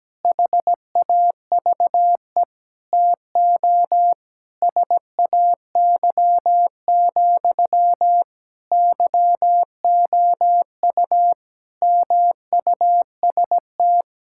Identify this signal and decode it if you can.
Morse code, heard as a keyed tone, about 17 wpm, 700 Hz. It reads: HAVE TO SAY, YOU MUST
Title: Sentences from Top 1600 Words 17wpm